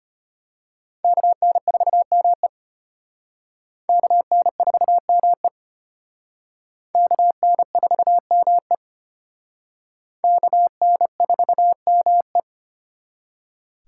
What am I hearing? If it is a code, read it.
Morse code, heard as a keyed tone, about 31 wpm, 700 Hz. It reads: KN4ME KN4ME KN4ME KN4ME
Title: Speed Racing - USA 2x2 Call Signs 25wpm